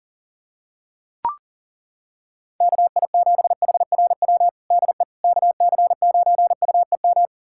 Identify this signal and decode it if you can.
Morse code, heard as a keyed tone, about 37 wpm, 700 Hz. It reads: KI7HRW DE KC9UEM
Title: USA Callsign DE Callsign - 6 Characters - 40wpm